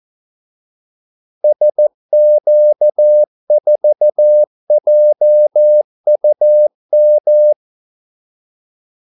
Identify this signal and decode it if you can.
Morse code, heard as a keyed tone, about 14 wpm, 600 Hz. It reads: SQ4JUM
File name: Callsigns-14wpm